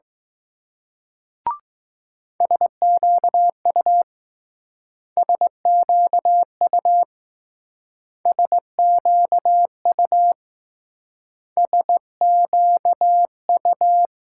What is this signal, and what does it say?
Morse code, heard as a keyed tone, about 20 wpm, 700 Hz. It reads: SQU SQU SQU SQU